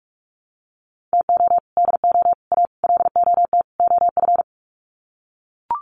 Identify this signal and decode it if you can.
Morse code, heard as a keyed tone, about 44 wpm, 700 Hz. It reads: TO DO A LOT OF
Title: Sets of 5 Words Encore 45wpm